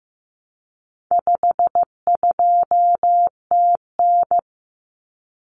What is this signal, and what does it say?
Morse code, heard as a keyed tone, about 15 wpm, 700 Hz. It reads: 52TN
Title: Common Cut Numbers - 4-Digit 15wpm